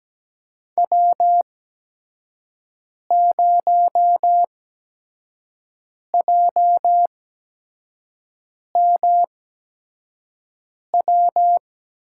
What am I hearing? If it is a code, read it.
Morse code, heard as a keyed tone, about 17 wpm, spaced as for 5 wpm, 700 Hz. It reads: W0JMW
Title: USA Call Signs - 8x Character Spacing - 17wpm